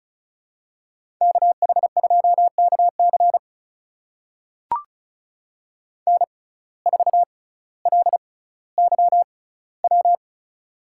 Morse code, heard as a keyed tone, about 36 wpm, spaced as for 18 wpm, 700 Hz. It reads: KH2KC N4LYW
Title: USA Call Signs - 5 Characters - 6x Character Spacing - 35wpm